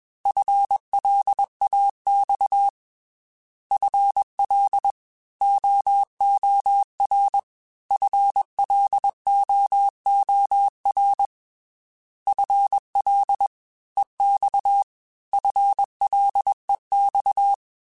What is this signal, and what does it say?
Morse code, heard as a keyed tone, about 21 wpm, 800 Hz. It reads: FLAX FL OOR FLOOR FL EX FLEX
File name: FL / FL mp3s / Fast FL @20